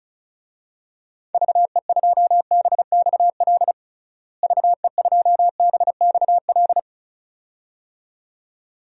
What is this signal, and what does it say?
Morse code, heard as a keyed tone, about 36 wpm, 700 Hz. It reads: VE2BXL VE2BXL
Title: DXCC-100 Call Signs - 6 Characters - Repeated 2x - 35wpm